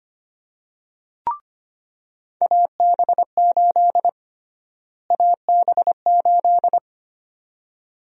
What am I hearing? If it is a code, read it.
Morse code, heard as a keyed tone, about 25 wpm, 700 Hz. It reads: AB8 AB8
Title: USA Call Sign Prefixes - Repeated 2x - 25wpm